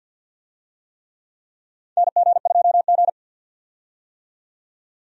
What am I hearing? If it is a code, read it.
Morse code, heard as a keyed tone, about 50 wpm, 700 Hz. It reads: NG2G
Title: USA Call Signs - 4 Characters - 50wpm